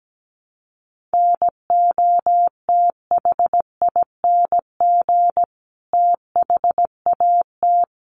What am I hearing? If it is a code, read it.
Morse code, heard as a keyed tone, about 17 wpm, 700 Hz. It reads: NOTHING THAT
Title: Sentences from Top 1100 Words 17wpm